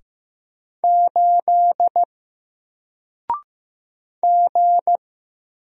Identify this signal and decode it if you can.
Morse code, heard as a keyed tone, about 15 wpm, 700 Hz. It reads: 8 G
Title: Single Letter-Number 15wpm